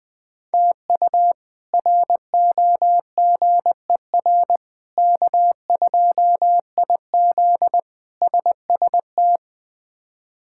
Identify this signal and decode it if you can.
Morse code, heard as a keyed tone, about 20 wpm, 700 Hz. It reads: TU ROGER K2IZ SST
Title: K1USN SST QSO Practice - 20wpm